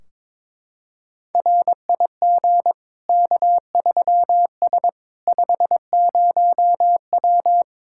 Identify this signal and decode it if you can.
Morse code, heard as a keyed tone, about 21 wpm, 700 Hz. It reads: RIG K3S 50W
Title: K1USN SST QSO Practice 22wpm